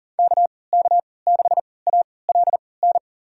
Morse code, heard as a keyed tone, about 38 wpm, spaced as for 22 wpm, 700 Hz. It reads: KK6ALN
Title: USA Call Signs - 6 Characters - 3x Character Spacing - 40wpm